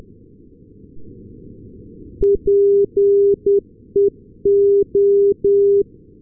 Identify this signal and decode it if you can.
Morse code, heard as a keyed tone, about 10 wpm, 400 Hz. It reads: PEO
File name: europe / s-280 ndb nykoping-peola peo-398, sm1otx